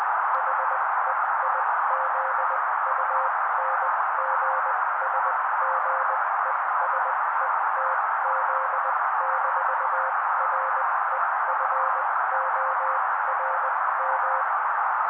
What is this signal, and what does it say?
Morse code, heard as a keyed tone, about 20 wpm, spaced as for 16 wpm, 530 Hz. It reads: HEIZUNGSGESETZ-REFORM
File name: daily / 16wpm noise